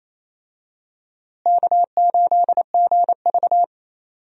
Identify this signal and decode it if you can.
Morse code, heard as a keyed tone, about 28 wpm, 700 Hz. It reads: K8GV
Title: USA Call Signs - 4 Characters - Repeated 2x - 28wpm